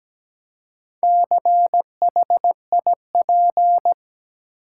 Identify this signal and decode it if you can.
Morse code, heard as a keyed tone, about 17 wpm, 700 Hz. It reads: CHIP
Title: Four Letter Words - 17wpm